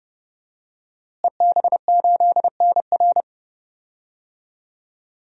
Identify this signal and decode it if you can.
Morse code, heard as a keyed tone, about 30 wpm, 700 Hz. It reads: EB8NR
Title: DXCC-100 Call Signs - 5 Characters - Repeated 2x - 30wpm